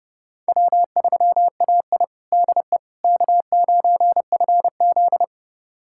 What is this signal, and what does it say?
Morse code, heard as a keyed tone, about 30 wpm, 700 Hz. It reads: W3AI DE K9FZ
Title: USA Callsign DE Callsign - 4 Characters - 30wpm